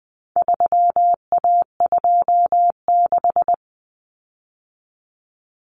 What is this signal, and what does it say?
Morse code, heard as a keyed tone, about 20 wpm, 700 Hz. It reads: 3A26
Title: Common Cut Numbers - 4-Digit 20wpm